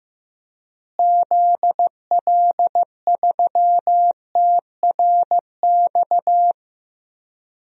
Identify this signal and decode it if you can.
Morse code, heard as a keyed tone, about 15 wpm, 700 Hz. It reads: ZL3TRX
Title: DXCC-100 Call Signs - 6 Characters - Repeated 3x - 15wpm